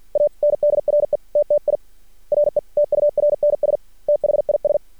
Morse code, heard as a keyed tone, about 53 wpm, spaced as for 45 wpm, 590 Hz. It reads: ANDRETTI RETURNS THIS
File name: archived / CRSnet